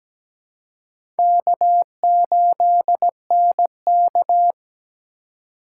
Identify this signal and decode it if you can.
Morse code, heard as a keyed tone, about 17 wpm, 700 Hz. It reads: K8NK
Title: USA Call Signs - 4 Characters - 3x Character Spacing - 17wpm